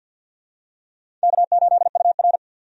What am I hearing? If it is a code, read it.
Morse code, heard as a keyed tone, about 50 wpm, 700 Hz. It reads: K8UR
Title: USA Call Signs - 4 Characters - 50wpm